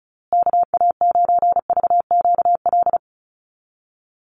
Morse code, heard as a keyed tone, about 34 wpm, 700 Hz. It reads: KA9VQL
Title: USA Call Signs Repeated 3x 35wpm